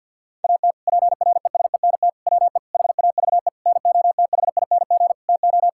Morse code, heard as a keyed tone, about 52 wpm, 700 Hz. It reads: AT PRESENT WE HAVE NOTHING TO
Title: Sentences from Top 1000 Words - Review Part 5 - 50wpm